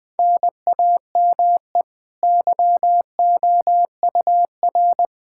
Morse code, heard as a keyed tone, about 20 wpm, 700 Hz. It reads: NAME YOUR
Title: Sentences from Top 1000 Words - Review Part 5 - 20wpm